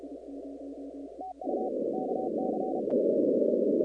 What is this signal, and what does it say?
Morse code, heard as a keyed tone, about 32 wpm, 750 Hz, with noise. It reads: TU KC
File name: contests / 2019-cw-wpx-kt5j-k5pi- -2330 0-41